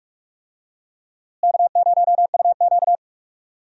Morse code, heard as a keyed tone, about 44 wpm, 700 Hz. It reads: K0UQ
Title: USA Call Signs - 4 Characters - Repeated 3x - 45wpm